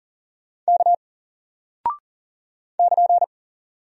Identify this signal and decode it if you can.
Morse code, heard as a keyed tone, about 40 wpm, 700 Hz. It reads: K <KN>
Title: Single Character (Letter:Number:Punctuation:Prosign) Alt 40wpm